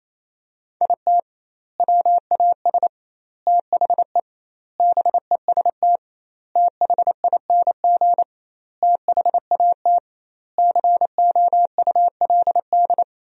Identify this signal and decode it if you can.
Morse code, heard as a keyed tone, about 28 wpm, 700 Hz. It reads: IT WAS THE BEST THING THAT COULD
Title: 2x Word-Spacing - Sentences from Top 400 Words - 28wpm